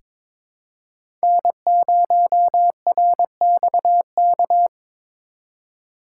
Morse code, heard as a keyed tone, about 21 wpm, 700 Hz. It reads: N0RXK
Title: USA Call Signs - 5 Characters - 2x Character Spacing - 22wpm